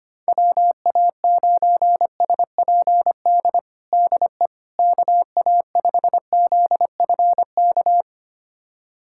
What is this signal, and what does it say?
Morse code, heard as a keyed tone, about 25 wpm, 700 Hz. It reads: WA9SPD DE KA5ZFK
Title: USA Callsign DE Callsign - 6 Characters - 25wpm